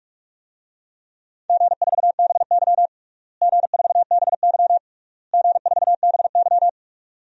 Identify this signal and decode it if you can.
Morse code, heard as a keyed tone, about 46 wpm, 700 Hz. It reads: G4BY G4BY G4BY
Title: DXCC-100 Call Signs - 4 Characters - Repeated 3x - 45wpm